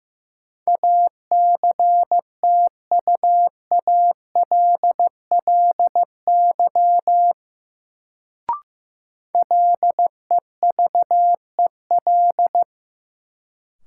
Morse code, heard as a keyed tone, about 15 wpm, 700 Hz. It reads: ACTUALLY LEVEL
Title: Sentences from Top 700 Words - New Word Drill - 15wpm